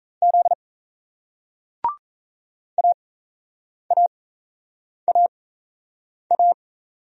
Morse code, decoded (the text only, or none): Z AAAA